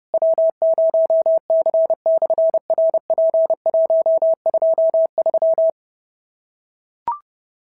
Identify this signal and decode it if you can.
Morse code, heard as a keyed tone, about 30 wpm, 690 Hz. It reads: W0C/RP123
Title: SOTA QSO Practice 30wpm